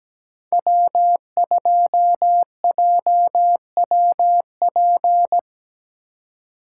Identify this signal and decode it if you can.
Morse code, heard as a keyed tone, about 17 wpm, 700 Hz. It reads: W2JWP
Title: USA Call Signs - 5 Characters - 3x Character Spacing - 17wpm